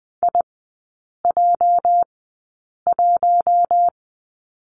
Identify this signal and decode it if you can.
Morse code, heard as a keyed tone, about 20 wpm, spaced as for 8 wpm, 700 Hz. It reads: IJ1
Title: 205 Lesson 33e - Letter j - Three Characters 20wpm